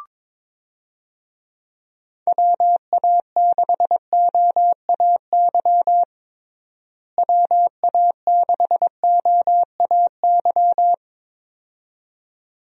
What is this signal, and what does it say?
Morse code, heard as a keyed tone, about 22 wpm, 700 Hz. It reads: WA6OAY WA6OAY
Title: USA Call Signs - 6 Characters - Repeated 2x - 22wpm